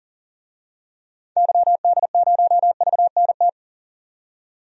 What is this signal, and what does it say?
Morse code, heard as a keyed tone, about 40 wpm, 700 Hz. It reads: YD0VNT
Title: DXCC-100 Call Signs - 6 Characters - 40wpm